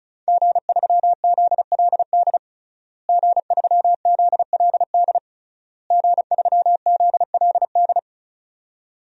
Morse code, heard as a keyed tone, about 34 wpm, 700 Hz. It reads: G3ZLD G3ZLD G3ZLD
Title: DXCC-100 Call Signs - 5 Characters - Repeated 3x - 35wpm